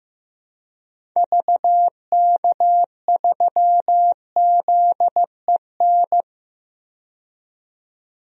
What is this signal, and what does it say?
Morse code, heard as a keyed tone, about 15 wpm, 690 Hz. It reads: VK3ZEN
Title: DXCC-100 Call Signs - 6 Characters - 15wpm